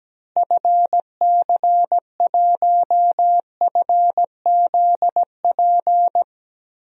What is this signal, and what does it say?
Morse code, heard as a keyed tone, about 17 wpm, 700 Hz. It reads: FC1FZP
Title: DXCC-100 Call Signs - 6 Characters - Repeated 3x - 17wpm